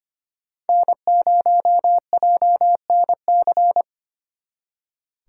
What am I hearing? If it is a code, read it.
Morse code, heard as a keyed tone, about 32 wpm, 700 Hz. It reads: N0JNC